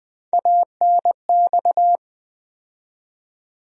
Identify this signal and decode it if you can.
Morse code, heard as a keyed tone, about 20 wpm, 700 Hz. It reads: ANX
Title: USA Call Sign Postfixes - 3 Letters 20wpm